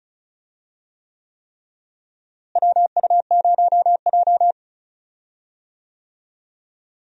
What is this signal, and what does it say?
Morse code, heard as a keyed tone, about 36 wpm, 700 Hz. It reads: WU0J
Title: USA Call Signs - 4 Characters - Repeated 3x - 35wpm